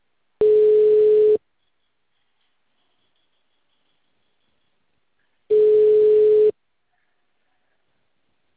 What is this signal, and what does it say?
Ringback tone (Europe, 425 Hz).
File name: out- -2026- - - .16722